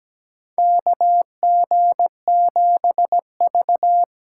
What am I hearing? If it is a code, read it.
Morse code, heard as a keyed tone, about 17 wpm, 700 Hz. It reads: KG7V
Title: USA Call Signs - 6 Characters - 8x Character Spacing - 17wpm